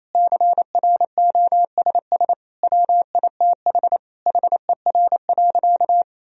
Morse code, heard as a keyed tone, about 28 wpm, 700 Hz. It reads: CROSS WITH HER.
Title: Sentences from Top 1000 Words - Review Part 1 - 28wpm